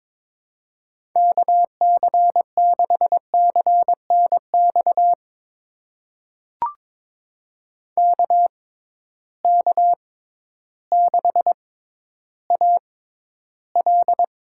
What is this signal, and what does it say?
Morse code, heard as a keyed tone, about 22 wpm, spaced as for 12 wpm, 700 Hz. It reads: KC6CNX KK6AL
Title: USA Call Signs - 6 Characters - 6x Character Spacing - 22wpm